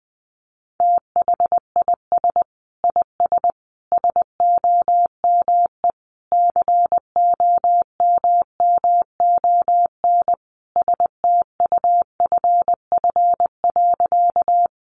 Morse code, heard as a keyed tone, about 20 wpm, 700 Hz. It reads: THIS IS SOME COMMON STUFF.
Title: Sentences from Top 1400 Words 20wpm